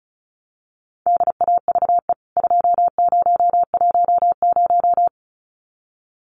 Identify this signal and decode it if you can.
Morse code, heard as a keyed tone, about 34 wpm, 700 Hz. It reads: DAVE 2010
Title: CWT Exchange 35wpm